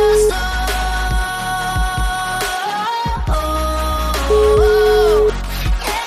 Ringback tone (Europe, 425 Hz), with noise.